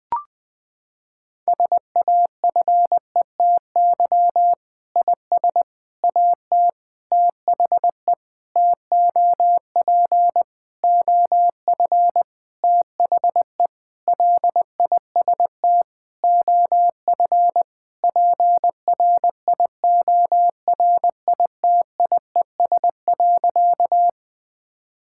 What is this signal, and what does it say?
Morse code, heard as a keyed tone, about 20 wpm, 700 Hz. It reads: SAFETY IS AT THE TOP OF THE LIST OF PRIORITIES.
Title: Sentences with 7-Word Phrases - 20wpm